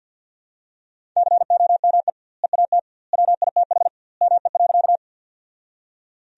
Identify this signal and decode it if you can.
Morse code, heard as a keyed tone, about 51 wpm, 700 Hz. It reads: COME EAT WITH ME.
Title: 2x Word-Spacing - Sentences from Top 300 Words - 50wpm